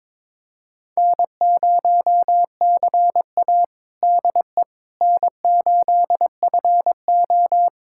Morse code, heard as a keyed tone, about 22 wpm, 700 Hz. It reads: N0CA DE N8FO
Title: USA Callsign DE Callsign - 4 Characters - 22wpm